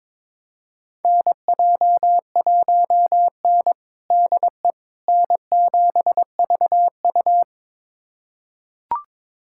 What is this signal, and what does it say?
Morse code, heard as a keyed tone, about 22 wpm, 700 Hz. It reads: NJ1N DE N7VU
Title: USA Callsign DE Callsign - 4 Characters - 22wpm